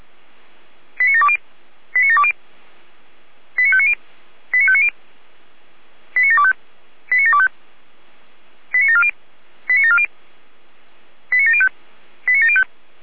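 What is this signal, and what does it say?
Five-tone selective call (ZVEI), heard as quick runs of five tones, tones of about 80 ms. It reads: 89720 89720 89590 89590 89725 89725 89740 89740 89705 89705
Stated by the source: ZVEI five-tone call received on SDR.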